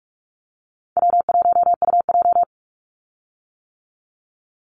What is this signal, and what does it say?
Morse code, heard as a keyed tone, about 44 wpm, 700 Hz. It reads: W1UJ
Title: Call Signs - 45wpm